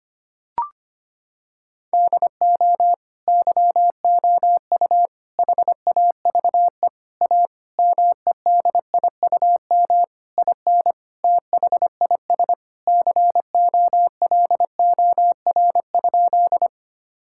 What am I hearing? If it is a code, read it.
Morse code, heard as a keyed tone, about 25 wpm, 700 Hz. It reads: DO YOU HAVE A MEDIUM IN THIS COLOR?
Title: Sentences from Top 1000 Words - Review Part 6 - 25wpm